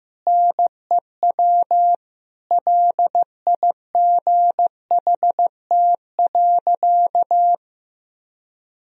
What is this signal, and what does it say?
Morse code, heard as a keyed tone, about 15 wpm, 700 Hz. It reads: NEW LIGHT.
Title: Sentences from Top 1000 Words - Review Part 1 - 15wpm